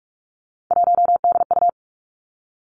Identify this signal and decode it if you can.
Morse code, heard as a keyed tone, about 46 wpm, 700 Hz. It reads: 1DU
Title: Three Challenging Letter-Number Combinations 45wpm